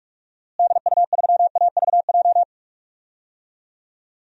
Morse code, heard as a keyed tone, about 46 wpm, 700 Hz. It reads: DU3AVJ
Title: DXCC-100 Call Signs - 6 Characters - Repeated 2x - 45wpm